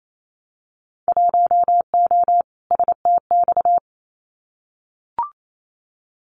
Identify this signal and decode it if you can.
Morse code, heard as a keyed tone, about 28 wpm, 700 Hz. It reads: 1O STX
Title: Winter Field Day Exchanges 28wpm